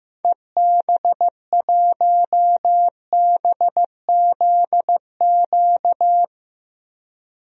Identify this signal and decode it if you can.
Morse code, heard as a keyed tone, about 15 wpm, 700 Hz. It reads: EB1BZQ
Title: DXCC-100 Call Signs - 6 Characters - 15wpm